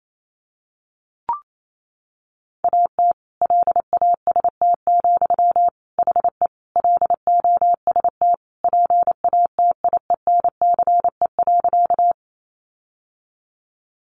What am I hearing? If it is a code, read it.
Morse code, heard as a keyed tone, about 28 wpm, 700 Hz. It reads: AT LAST, HE LOST PATIENCE.
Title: Sentences from Top 1400 Words 28wpm